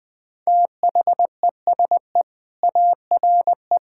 Morse code, heard as a keyed tone, about 20 wpm, 700 Hz. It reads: THESE ARE
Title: Sentences with 4-Word Phrases - 20wpm